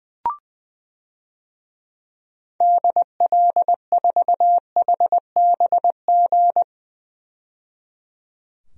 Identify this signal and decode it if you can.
Morse code, heard as a keyed tone, about 20 wpm, 700 Hz. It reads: DL4HBG
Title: DXCC-100 Call Signs - 6 Characters - 20wpm